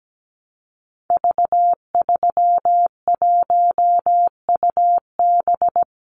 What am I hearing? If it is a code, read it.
Morse code, heard as a keyed tone, about 17 wpm, 700 Hz. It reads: V31UB
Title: Call Signs - 17wpm